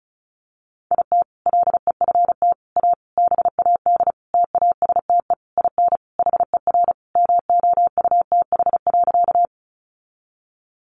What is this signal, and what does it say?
Morse code, heard as a keyed tone, about 34 wpm, 700 Hz. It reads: IT LEFT A BAD TASTE IN HER MOUTH.
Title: Sentences from Top 1500 Words 35wpm